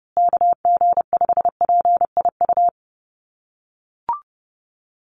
Morse code, heard as a keyed tone, about 30 wpm, 700 Hz. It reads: KG5PIU
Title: USA Call Signs Repeated 3x 30wpm